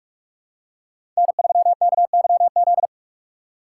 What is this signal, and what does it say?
Morse code, heard as a keyed tone, about 42 wpm, 700 Hz. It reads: N3KYZ
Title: USA Call Signs - 5 Characters - 6x Character Spacing - 45wpm